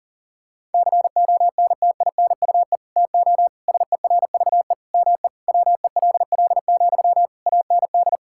Morse code, heard as a keyed tone, about 40 wpm, 700 Hz. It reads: CONTINUE TO SERVE ME WELL, AND
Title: Sentences from Top 1000 Words - Review Part 6 - 40wpm